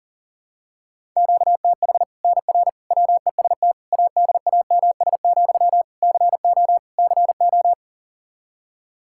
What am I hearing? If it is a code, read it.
Morse code, heard as a keyed tone, about 40 wpm, 700 Hz. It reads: QTH NR WEST ADAMS, CO CO
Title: Practice QSO - Exchange 1 - Beginning - 40wpm